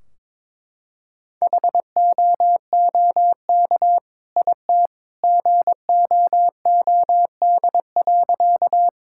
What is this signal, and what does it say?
Morse code, heard as a keyed tone, about 22 wpm, 700 Hz. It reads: HOOK IT GOOD.
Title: Sentences from Top 1600 Words 22wpm